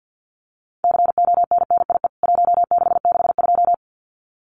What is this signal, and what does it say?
Morse code, heard as a keyed tone, about 49 wpm, 700 Hz. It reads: CONNIE 1662
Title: CWT Exchange 50wpm